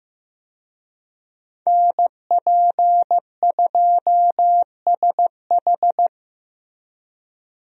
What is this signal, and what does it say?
Morse code, heard as a keyed tone, about 15 wpm, 700 Hz. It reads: NP2SH
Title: USA Call Signs - 5 Characters - Repeated 2x - 15wpm